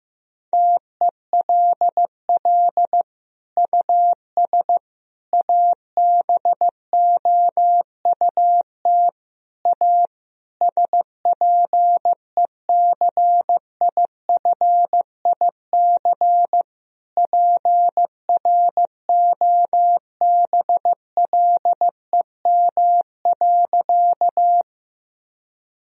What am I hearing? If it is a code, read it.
Morse code, heard as a keyed tone, about 15 wpm, 700 Hz. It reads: TELL US ABOUT A SPECIFIC PROBLEM.
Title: Sentences from Top 1000 Words - Review Part 3 - 15wpm